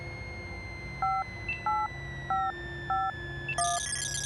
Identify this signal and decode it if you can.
DTMF, heard as telephone touch tones, tones of about 220 ms. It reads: 58551